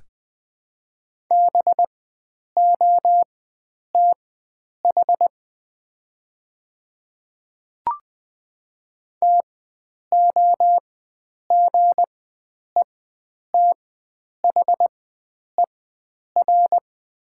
Morse code, heard as a keyed tone, about 20 wpm, spaced as for 9 wpm, 700 Hz. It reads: BOTH TOGETHER
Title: 4x Character Spacing - Top 300 Words - 20wpm